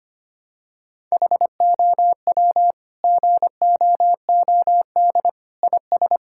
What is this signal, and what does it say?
Morse code, heard as a keyed tone, about 25 wpm, 700 Hz. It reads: HOW GOOD IS
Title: Sentences from Top 1000 Words - Review Part 6 - 25wpm